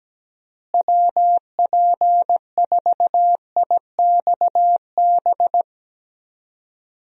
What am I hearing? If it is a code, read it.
Morse code, heard as a keyed tone, about 17 wpm, 700 Hz. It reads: WP4IXB
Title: USA Call Signs - 6 Characters - Repeated 2x - 17wpm